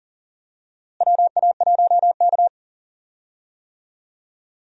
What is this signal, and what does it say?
Morse code, heard as a keyed tone, about 37 wpm, 700 Hz. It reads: WA1K
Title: USA Call Signs - 4 Characters - Repeated 3x - 40wpm